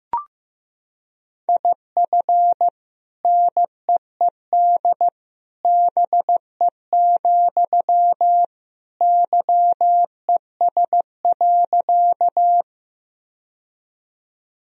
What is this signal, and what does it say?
Morse code, heard as a keyed tone, about 15 wpm, 700 Hz. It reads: IF NEED BE, YES.
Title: Sentences from Top 1000 Words - Review Part 1 - 15wpm